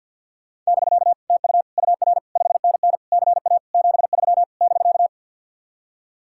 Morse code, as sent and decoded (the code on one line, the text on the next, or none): -...-.- / - ..- / ..- .-. / ..... -. -. / -.-. .- / --... ...-- / -...-.-
<BK> TU UR 5NN CA 73 <BK>